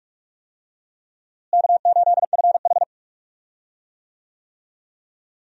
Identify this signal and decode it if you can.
Morse code, heard as a keyed tone, about 41 wpm, 700 Hz. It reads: K8FH
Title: USA Call Signs - 4 Characters - Repeated 3x - 45wpm